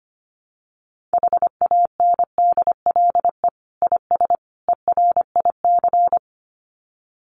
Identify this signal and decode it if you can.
Morse code, heard as a keyed tone, about 25 wpm, 700 Hz. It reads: HANDLE IS ERIC
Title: Common QSO Phrases 25wpm